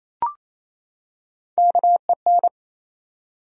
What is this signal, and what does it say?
Morse code, heard as a keyed tone, about 28 wpm, 700 Hz. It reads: KEN
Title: USA Common Names - Max 3 Letters - 28wpm